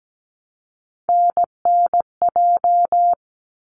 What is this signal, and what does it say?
Morse code, heard as a keyed tone, about 17 wpm, 700 Hz. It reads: NNJ
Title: ARRL Sections 17wpm